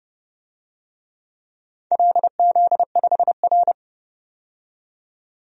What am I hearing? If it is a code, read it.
Morse code, heard as a keyed tone, about 28 wpm, 700 Hz. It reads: LZ5R
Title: DXCC-100 Call Signs - 4 Characters - 30wpm